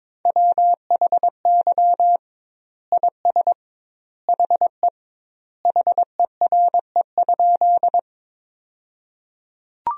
Morse code, heard as a keyed tone, about 22 wpm, 700 Hz. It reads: WHY IS HE HERE?
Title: 2x Word-Spacing - Sentences from Top 200 Words - 22wpm